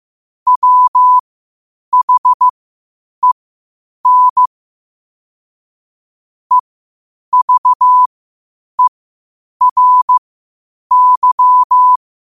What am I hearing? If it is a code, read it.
Morse code, heard as a keyed tone, about 15 wpm, spaced as for 8 wpm, 1000 Hz. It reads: WHEN EVERY
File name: 8wpm